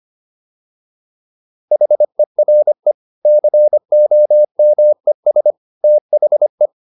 Morse code, heard as a keyed tone, about 25 wpm, 600 Hz. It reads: HERE COMES THE
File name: PR405 25